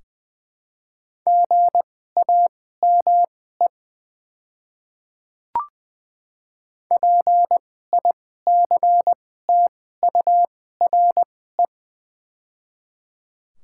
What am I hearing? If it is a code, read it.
Morse code, heard as a keyed tone, about 20 wpm, spaced as for 15 wpm, 700 Hz. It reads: GAME PICTURE
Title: 2x Character Spacing - Top 500 Words - 20wpm